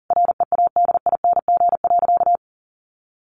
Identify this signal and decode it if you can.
Morse code, heard as a keyed tone, about 40 wpm, 700 Hz. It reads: READING.
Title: Sentences from Top 1000 Words 40wpm